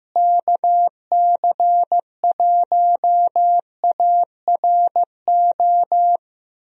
Morse code, heard as a keyed tone, about 15 wpm, 700 Hz. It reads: KC1ARO
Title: USA Call Signs - 6 Characters - Repeated 3x - 15wpm